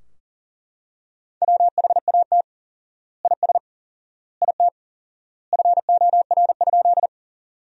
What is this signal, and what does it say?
Morse code, heard as a keyed tone, about 40 wpm, 700 Hz. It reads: WHAT IS IT FOR?
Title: 4x Word-Spacing - Sentences from Top 100 Words - 40wpm